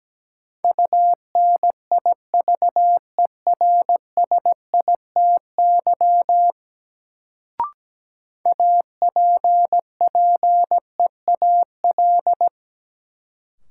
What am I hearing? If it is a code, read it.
Morse code, heard as a keyed tone, about 17 wpm, 700 Hz. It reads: UNIVERSITY APPEAL
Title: Sentences from Top 1300 Words - New Word Drill Alt - 17wpm